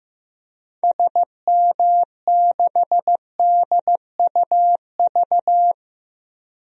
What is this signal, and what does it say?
Morse code, heard as a keyed tone, about 15 wpm, 700 Hz. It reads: SM6DUV
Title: DXCC-100 Call Signs - 6 Characters - 15wpm